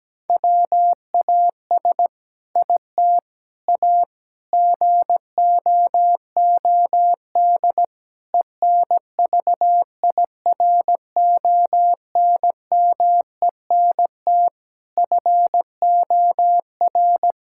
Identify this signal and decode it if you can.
Morse code, heard as a keyed tone, about 17 wpm, 700 Hz. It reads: WAS IT A GOOD ENVIRONMENT FOR
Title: Sentences from Top 1000 Words - Review Part 5 - 17wpm